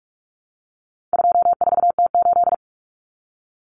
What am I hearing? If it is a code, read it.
Morse code, heard as a keyed tone, about 44 wpm, 700 Hz. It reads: 24T8
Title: Common Cut Numbers - 4-Digit 45wpm